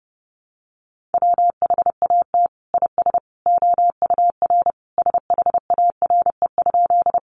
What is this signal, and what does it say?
Morse code, heard as a keyed tone, about 30 wpm, 700 Hz. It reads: WHAT IS OUR SHARE?
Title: Sentences from Top 900 Words 30wpm